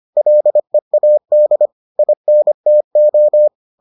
Morse code, heard as a keyed tone, about 25 wpm, 600 Hz. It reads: LEAD INTO